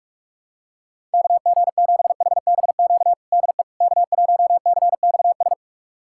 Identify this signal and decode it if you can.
Morse code, heard as a keyed tone, about 46 wpm, 700 Hz. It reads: KG7HBQ DE K1CXS
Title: USA Callsign DE Callsign 45wpm